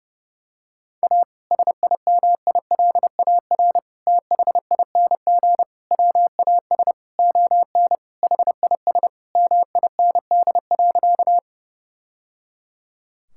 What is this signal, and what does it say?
Morse code, heard as a keyed tone, about 30 wpm, 700 Hz. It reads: A SIMILAR THING WAS ON HIS MIND.
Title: Sentences from Top 1000 Words - Review Part 5 - 30wpm